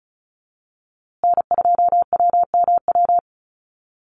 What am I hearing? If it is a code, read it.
Morse code, heard as a keyed tone, about 34 wpm, 700 Hz. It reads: N2WMW
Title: USA Call Signs Repeated 2x 35wpm